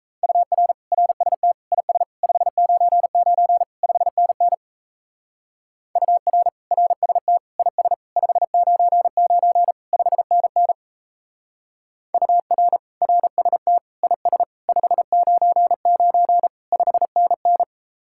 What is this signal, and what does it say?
Morse code, heard as a keyed tone, about 37 wpm, 700 Hz. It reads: UR RST IS 599 5NN UR RST IS 599 5NN UR RST IS 599 5NN